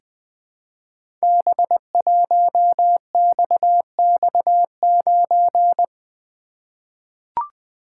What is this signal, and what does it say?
Morse code, heard as a keyed tone, about 20 wpm, 700 Hz. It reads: B1XX9
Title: Code Groups - 5 Letter:Numbers - 20wpm